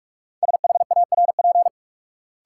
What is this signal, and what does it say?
Morse code, heard as a keyed tone, about 46 wpm, 700 Hz. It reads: SHARP
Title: Sentences from Top 1600 Words - New Word Drill - 45wpm